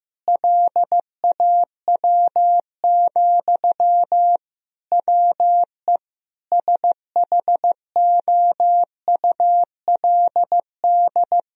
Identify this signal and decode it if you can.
Morse code, heard as a keyed tone, about 15 wpm, 700 Hz. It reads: LAW, WE SHOULD
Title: Sentences from Top 1000 Words - Review Part 4 - 15wpm